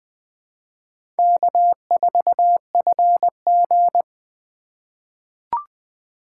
Morse code, heard as a keyed tone, about 20 wpm, 700 Hz. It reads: K4FG
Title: USA Call Signs - 4 Characters - 2x Character Spacing - 20wpm